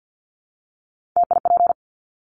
Morse code, heard as a keyed tone, about 50 wpm, 700 Hz. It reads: TIP
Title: Top 1501-1750 Words 50wpm